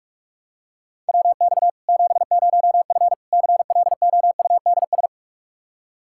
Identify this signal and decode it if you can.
Morse code, heard as a keyed tone, about 46 wpm, 700 Hz. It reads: WX 70F CLOUDS
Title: Practice QSO - Exchange 2 - Info - 45wpm